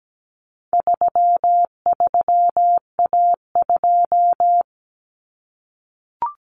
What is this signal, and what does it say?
Morse code, heard as a keyed tone, about 17 wpm, 700 Hz. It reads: 33A2
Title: Common Cut Numbers - 4-Digit 17wpm